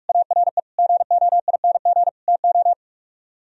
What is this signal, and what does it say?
Morse code, heard as a keyed tone, about 62 wpm, spaced as for 53 wpm, 700 Hz. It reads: ARE GOING TO
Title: Speed-Racing - Sets of 4 Words 45wpm